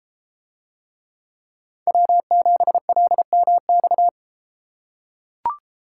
Morse code, heard as a keyed tone, about 32 wpm, 700 Hz. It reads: W7LMX